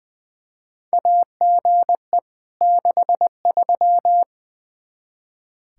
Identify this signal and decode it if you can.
Morse code, heard as a keyed tone, about 21 wpm, 700 Hz. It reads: AGE 63
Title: Practice QSO - Exchange 3 - More Info - 20wpm